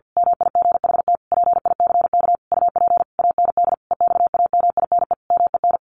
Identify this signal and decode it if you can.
Morse code, heard as a keyed tone, about 50 wpm, 700 Hz. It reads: MIGHT PICK UP AND EXAMINE MEN
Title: A Sound Of Thunder 50wpm